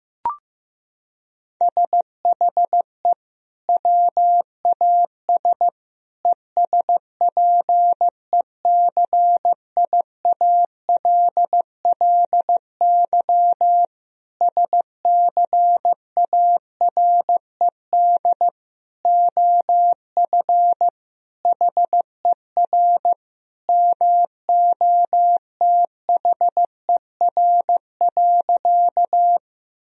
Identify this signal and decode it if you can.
Morse code, heard as a keyed tone, about 16 wpm, 700 Hz. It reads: SHE WAS ESPECIALLY SCARED OF HER MOTHER.
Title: Sentences from Top 1000 Words - Review Part 4 - 15wpm